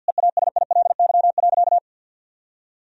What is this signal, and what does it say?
Morse code, heard as a keyed tone, about 46 wpm, 700 Hz. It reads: EASILY.
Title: Sentences from Top 1000 Words - Review Part 3 - 50wpm